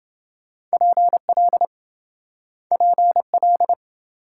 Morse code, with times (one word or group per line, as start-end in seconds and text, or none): 0.7-1.6 PL
2.7-3.7 PL